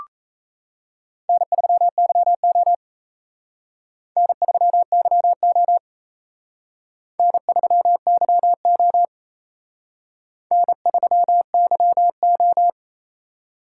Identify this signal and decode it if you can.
Morse code, heard as a keyed tone, about 33 wpm, 700 Hz. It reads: N3YO N3YO N3YO N3YO
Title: Speed Racing - USA 1x2 Call Signs 28wpm